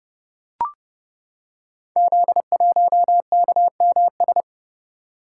Morse code, heard as a keyed tone, about 30 wpm, 700 Hz. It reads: Z1KMS
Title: Code Groups - 5 Letter:Numbers - 30wpm